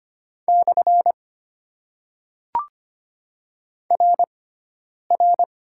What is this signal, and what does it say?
Morse code, heard as a keyed tone, about 25 wpm, spaced as for 9 wpm, 700 Hz. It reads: / RR
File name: Top 90 QSO Elements - 6x Character Spacing - 25wpm